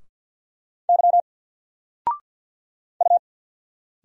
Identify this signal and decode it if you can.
Morse code, heard as a keyed tone, about 50 wpm, 700 Hz. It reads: Y U
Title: Single Letter-Number Alt 50wpm